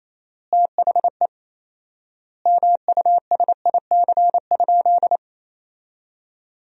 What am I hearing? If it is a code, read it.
Morse code, heard as a keyed tone, about 28 wpm, 700 Hz. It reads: THE MUSIC?
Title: 4x Word-Spacing - Sentences from Top 300 Words - 28wpm